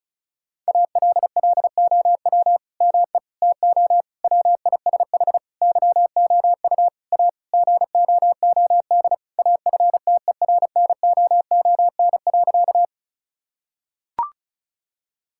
Morse code, heard as a keyed tone, about 36 wpm, 700 Hz. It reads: ALLOW ME TO WISH YOU A GOOD AFTERNOON.
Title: Sentences from Top 1600 Words 35wpm